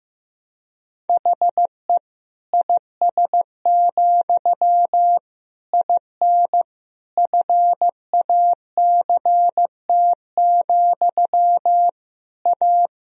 Morse code, heard as a keyed tone, about 15 wpm, 700 Hz. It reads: HE IS, IN FACT, A
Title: Sentences from Top 1000 Words - Review Part 3 - 15wpm